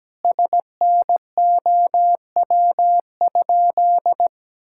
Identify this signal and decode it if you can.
Morse code, heard as a keyed tone, about 17 wpm, 700 Hz. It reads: SNOW?
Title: Sentences from Top 1600 Words 17wpm